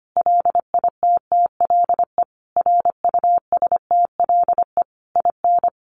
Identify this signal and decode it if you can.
Morse code, heard as a keyed tone, about 25 wpm, 700 Hz. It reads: LITTLE RUSTLE IN
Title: A Sound Of Thunder 25wpm